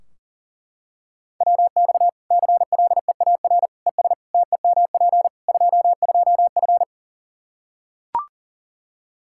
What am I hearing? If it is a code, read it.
Morse code, heard as a keyed tone, about 40 wpm, 700 Hz. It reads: WX CLEAR ES TEMP 22F
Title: Practice QSO - Exchange 2 - Info - 40wpm